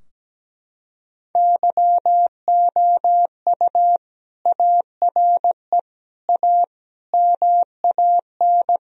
Morse code, heard as a keyed tone, about 17 wpm, 700 Hz. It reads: YOU ARE A MAN
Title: Sentences from Top 1000 Words - Review Part 3 - 17wpm